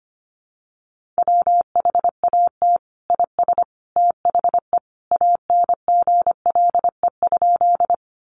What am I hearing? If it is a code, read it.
Morse code, heard as a keyed tone, about 25 wpm, 700 Hz. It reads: WHAT IS THE ANGLE?
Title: Sentences from Top 1500 Words 25wpm